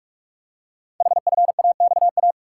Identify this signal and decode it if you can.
Morse code, heard as a keyed tone, about 44 wpm, 700 Hz. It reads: HFAXA
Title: Code Groups - 5 Letters - 45wpm